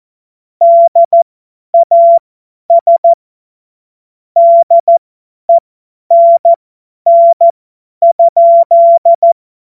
Morse code, heard as a keyed tone, about 14 wpm, spaced as for 10 wpm, 680 Hz. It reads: DAS DENN?
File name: Morsekurs / Witze / Witz 02. Februar 2021, 50 F70